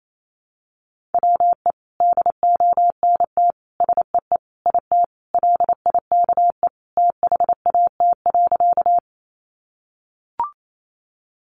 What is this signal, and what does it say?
Morse code, heard as a keyed tone, about 28 wpm, 700 Hz. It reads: WE DONT SEE IT LIKE THAT.
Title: Sentences from Top 300 Words 28wpm